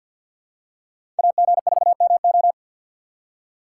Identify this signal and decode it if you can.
Morse code, heard as a keyed tone, about 52 wpm, 700 Hz. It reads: AG4MO
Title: USA Call Signs - 5 Characters - 2x Character Spacing - 50wpm